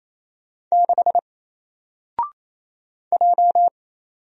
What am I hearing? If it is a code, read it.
Morse code, heard as a keyed tone, about 29 wpm, 700 Hz. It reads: 6 J
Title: Single Letter-Number 28wpm